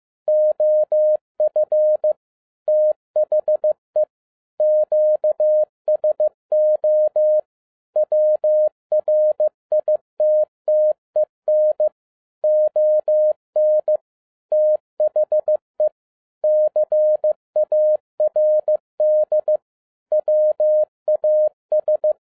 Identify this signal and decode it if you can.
Morse code, heard as a keyed tone, about 15 wpm, 610 Hz. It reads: OF THE QSO WRITTEN ON THE CARD WAS
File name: 2011-10-mp3 / n07-A DOT 15wpm.mp3